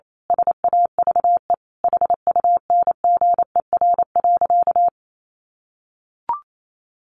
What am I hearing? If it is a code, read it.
Morse code, heard as a keyed tone, about 28 wpm, 700 Hz. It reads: SAVE HUNGER.
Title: A Sound Of Thunder 28wpm